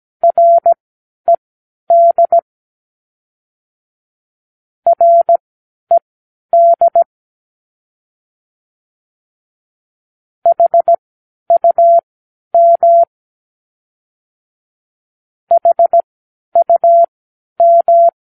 Morse code, heard as a keyed tone, about 17 wpm, spaced as for 10 wpm, 680 Hz. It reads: RED RED HUM HUM
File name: k7qo / Practicewords